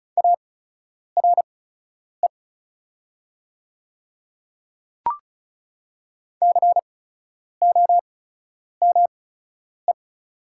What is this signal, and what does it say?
Morse code, heard as a keyed tone, about 36 wpm, spaced as for 10 wpm, 690 Hz. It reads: ARE COME
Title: 8x Character Spacing - Top 200 Words - 35wpm